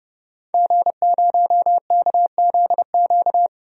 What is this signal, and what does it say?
Morse code, heard as a keyed tone, about 30 wpm, 700 Hz. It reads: G0KZQ
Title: DXCC-100 Call Signs - 5 Characters - 30wpm